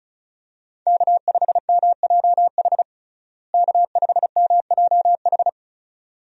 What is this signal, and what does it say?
Morse code, heard as a keyed tone, about 34 wpm, 700 Hz. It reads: K5MJH K5MJH
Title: USA Call Signs - 5 Characters - Repeated 2x - 35wpm